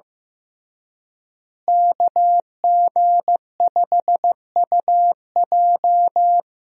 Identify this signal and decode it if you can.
Morse code, heard as a keyed tone, about 16 wpm, 700 Hz. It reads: KG5UJ
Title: USA Call Signs - 5 Characters - Repeated 3x - 15wpm